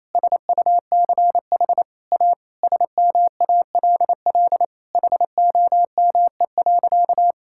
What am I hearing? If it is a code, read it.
Morse code, heard as a keyed tone, about 28 wpm, 700 Hz. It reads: SUCH A SMALL HOME.
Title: Sentences from Top 1000 Words - Review Part 6 - 28wpm